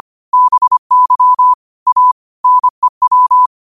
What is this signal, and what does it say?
Morse code, heard as a keyed tone, about 25 wpm, 1000 Hz. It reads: BY A NEW